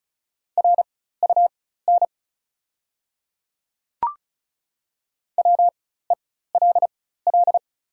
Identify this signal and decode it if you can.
Morse code, heard as a keyed tone, about 35 wpm, spaced as for 16 wpm, 700 Hz. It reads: RUN WELL
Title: 4x Character Spacing - Top 500 Words - 35wpm